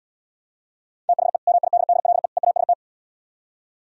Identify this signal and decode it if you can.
Morse code, heard as a keyed tone, about 77 wpm, 700 Hz. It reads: THE MIDDLE EAST